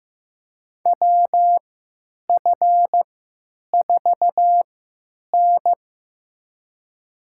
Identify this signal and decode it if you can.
Morse code, heard as a keyed tone, about 15 wpm, spaced as for 9 wpm, 700 Hz. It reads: WF4N
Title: USA Call Signs - 4 Characters - 3x Character Spacing - 15wpm